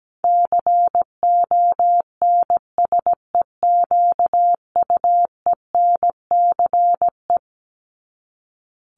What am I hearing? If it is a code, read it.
Morse code, heard as a keyed tone, about 17 wpm, 700 Hz. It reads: CONSEQUENCE
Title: LY Suffix - Top 100 Words with Base Words 17wpm